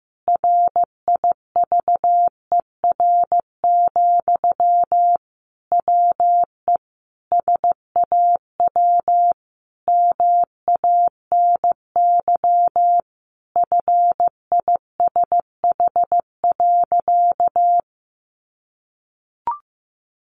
Morse code, heard as a keyed tone, about 15 wpm, 700 Hz. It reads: RIVER, WE SAW MANY FISH.
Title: Sentences from Top 1500 Words 15wpm